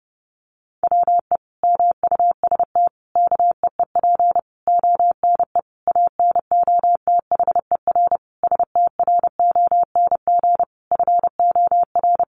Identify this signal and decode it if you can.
Morse code, heard as a keyed tone, about 30 wpm, 700 Hz. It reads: WE MUST KEEP ONE ANOTHER STRONG FOR
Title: Sentences from Top 600 Words 30wpm